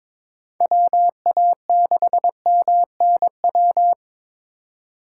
Morse code, heard as a keyed tone, about 23 wpm, 700 Hz. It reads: WA6MNW
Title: USA Call Signs - 6 Characters - 3x Character Spacing - 22wpm